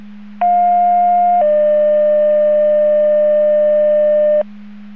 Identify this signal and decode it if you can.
Two-tone page, heard as a one-second tone, then a three-second tone, about 710 then 600 Hz. It is heard from 0.4 to 4.4 s.